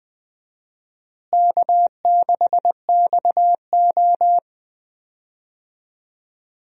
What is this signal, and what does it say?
Morse code, heard as a keyed tone, about 20 wpm, 700 Hz. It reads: K6XO
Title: USA Call Signs - 4 Characters - Repeated 3x - 20wpm